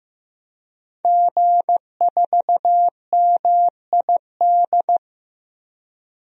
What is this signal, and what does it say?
Morse code, heard as a keyed tone, about 15 wpm, 700 Hz. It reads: G4MID
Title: DXCC-100 Call Signs - 5 Characters - Repeated 3x - 15wpm